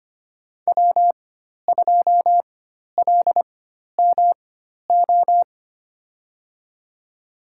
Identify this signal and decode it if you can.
Morse code, heard as a keyed tone, about 25 wpm, spaced as for 12 wpm, 700 Hz. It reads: W2LMO
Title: USA Call Signs - 5 Characters - 4x Character Spacing - 25wpm